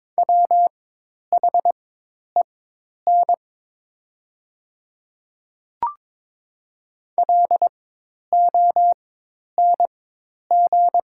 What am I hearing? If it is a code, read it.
Morse code, heard as a keyed tone, about 22 wpm, spaced as for 10 wpm, 700 Hz. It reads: WHEN LONG
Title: 4x Character Spacing - Top 100 Words - 22wpm